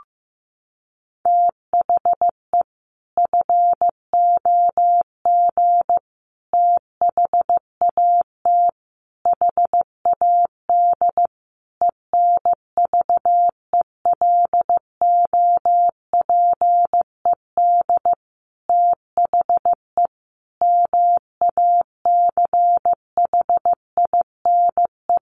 Morse code, heard as a keyed tone, about 15 wpm, 700 Hz. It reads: THE FOG THAT HAD ENVELOPED THE MACHINE
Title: A Sound Of Thunder 15wpm